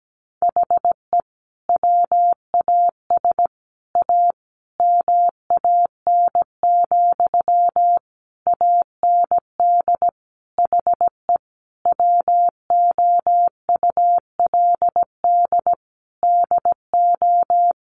Morse code, heard as a keyed tone, about 17 wpm, 700 Hz. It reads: HE WAS A MAN, AND HE WOULD DO
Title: Sentences from Top 1500 Words 17wpm